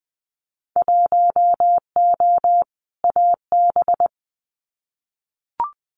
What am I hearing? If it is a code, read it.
Morse code, heard as a keyed tone, about 20 wpm, 700 Hz. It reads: 1O AB
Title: Winter Field Day Exchanges Encore 20wpm